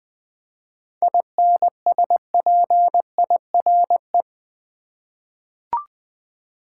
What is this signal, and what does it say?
Morse code, heard as a keyed tone, about 20 wpm, 700 Hz. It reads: INSPIRE
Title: 7 Letter Words - First 3 Letters Given - 20wpm